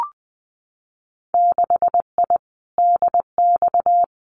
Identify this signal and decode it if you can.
Morse code, heard as a keyed tone, about 20 wpm, 700 Hz. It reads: 6I DX
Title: Winter Field Day Exchanges Encore 20wpm